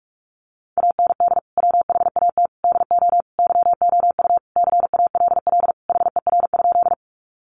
Morse code, heard as a keyed tone, about 44 wpm, 700 Hz. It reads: AND WHAT DO YOU CALL HER?
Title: Sentences from Top 100 Words 45wpm